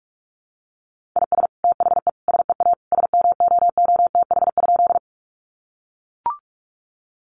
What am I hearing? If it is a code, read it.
Morse code, heard as a keyed tone, about 44 wpm, 700 Hz. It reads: IS THE SEA SMOOTH?
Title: Sentences from Top 1400 Words 45wpm